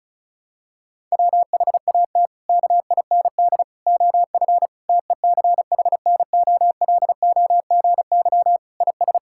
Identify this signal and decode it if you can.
Morse code, heard as a keyed tone, about 36 wpm, 700 Hz. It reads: WHAT KIND OF TECHNOLOGY IS
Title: Sentences from Top 1000 Words - Review Part 5 - 35wpm